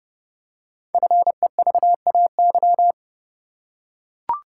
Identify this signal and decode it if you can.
Morse code, heard as a keyed tone, about 30 wpm, 700 Hz. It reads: FEVAY
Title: Code Groups - 5 Letter:Numbers - 30wpm